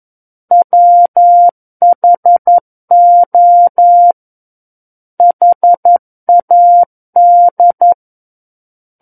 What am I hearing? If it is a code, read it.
Morse code, heard as a keyed tone, about 11 wpm, 700 Hz. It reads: WHO HAD
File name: 11 WPM